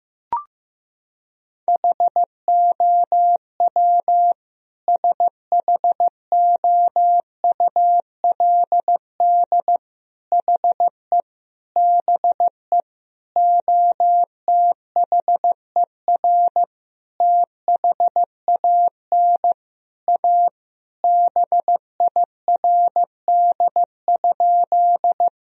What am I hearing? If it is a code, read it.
Morse code, heard as a keyed tone, about 15 wpm, 700 Hz. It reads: HOW SHOULD HE BE OTHER THAN A BIRD?
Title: Sentences from Top 1000 Words - Review Part 2 - 15wpm